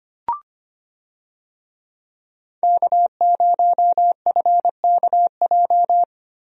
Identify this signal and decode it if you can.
Morse code, heard as a keyed tone, about 25 wpm, 690 Hz. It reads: K0FKJ
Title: USA Call Signs - 5 Characters - Repeated 2x - 25wpm